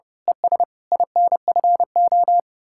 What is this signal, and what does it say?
Morse code, heard as a keyed tone, about 30 wpm, 700 Hz. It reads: ES INFO
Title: Speed-Racing Common QSO Phrases - 30wpm